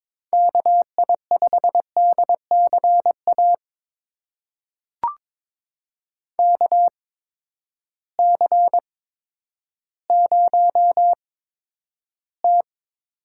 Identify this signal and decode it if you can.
Morse code, heard as a keyed tone, about 21 wpm, spaced as for 11 wpm, 700 Hz. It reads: KI5DCA KC0T
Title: USA Call Signs - 8x Character Spacing - 22wpm